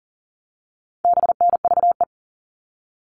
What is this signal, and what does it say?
Morse code, heard as a keyed tone, about 40 wpm, 700 Hz. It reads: BNVE
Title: All Cut Numbers - 4-Digit 40wpm